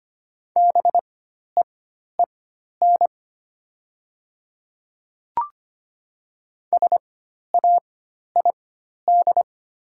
Morse code, heard as a keyed tone, about 25 wpm, spaced as for 12 wpm, 700 Hz. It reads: BEEN SAID
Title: 4x Character Spacing - Top 200 Words - 25wpm